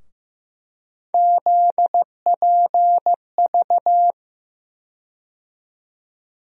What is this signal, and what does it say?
Morse code, heard as a keyed tone, about 15 wpm, 700 Hz. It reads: ZPV
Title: USA Call Sign Postfixes - 3 Letters 15wpm